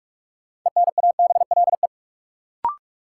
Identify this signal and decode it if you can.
Morse code, heard as a keyed tone, about 42 wpm, 700 Hz. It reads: ENABLE
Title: Sentences from Top 1500 Words - New Word Drill Alt - 45wpm